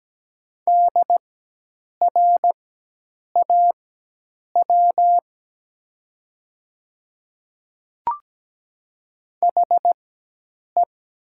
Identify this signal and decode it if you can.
Morse code, heard as a keyed tone, about 17 wpm, spaced as for 8 wpm, 700 Hz. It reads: DRAW HE
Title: 4x Character Spacing - Top 500 Words - 17wpm